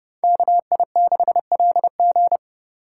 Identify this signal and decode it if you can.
Morse code, heard as a keyed tone, about 28 wpm, 700 Hz. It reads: KI6LG